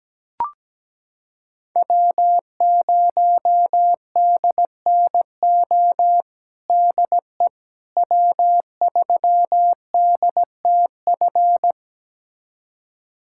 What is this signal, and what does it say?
Morse code, heard as a keyed tone, about 17 wpm, 700 Hz. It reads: W0DNO DE W3DTF
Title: USA Callsign DE Callsign - 5 Characters - 17wpm